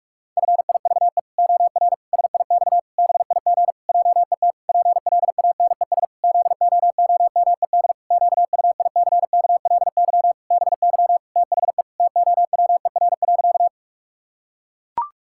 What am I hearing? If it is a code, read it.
Morse code, heard as a keyed tone, about 44 wpm, 700 Hz. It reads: FIVE OR SIX BIG JET PLANES ZOOMED QUICKLY BY THE TOWER.
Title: Pangram Challenge v2 - 45wpm